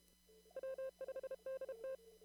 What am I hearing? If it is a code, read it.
Morse code, heard as a keyed tone, about 31 wpm, 530 Hz. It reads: W5DT
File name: CW / kt5j wpx cw-op-k5pi